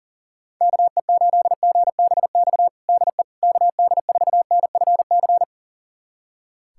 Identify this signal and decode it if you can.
Morse code, heard as a keyed tone, about 40 wpm, 700 Hz. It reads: KE8GBX DE KD4NFC
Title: USA Callsign DE Callsign - 6 Characters - 40wpm